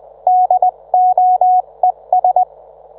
Morse code, heard as a keyed tone, about 20 wpm, 730 Hz. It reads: DOES